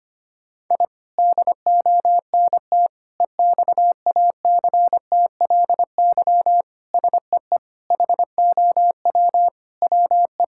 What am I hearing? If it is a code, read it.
Morse code, heard as a keyed tone, about 25 wpm, 700 Hz. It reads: I DONT EXACTLY SEE HOW WE
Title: Sentences from Top 1000 Words - Review Part 6 - 25wpm